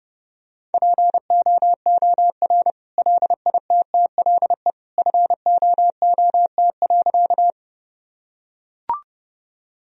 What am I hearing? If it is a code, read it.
Morse code, heard as a keyed tone, about 29 wpm, 700 Hz. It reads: POOR LITTLE FOOT.
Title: Sentences from Top 1000 Words - Review Part 3 - 30wpm